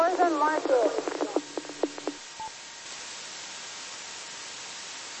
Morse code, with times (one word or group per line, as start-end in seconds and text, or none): none